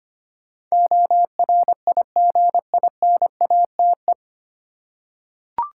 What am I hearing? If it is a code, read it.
Morse code, heard as a keyed tone, about 25 wpm, 690 Hz. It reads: ORIGINATE
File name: practice / 25wpm / 9 Letter Words - First 3 Letters Given - 25wpm